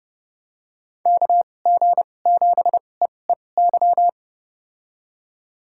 Morse code, heard as a keyed tone, about 30 wpm, spaced as for 22 wpm, 700 Hz. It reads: KG7EEY
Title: USA Call Signs - 2x Character Spacing - 30wpm